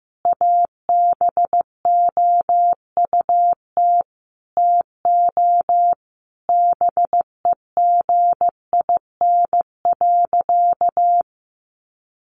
Morse code, heard as a keyed tone, about 15 wpm, 700 Hz. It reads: ABOUT TO BEGIN.
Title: Sentences from Top 1200 Words 15wpm